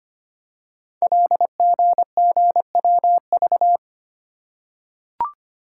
Morse code, heard as a keyed tone, about 25 wpm, 700 Hz. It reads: LGGWV
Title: Code Groups - 5 Letters - 25wpm